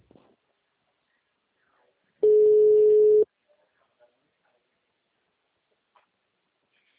Ringback tone (Europe, 425 Hz).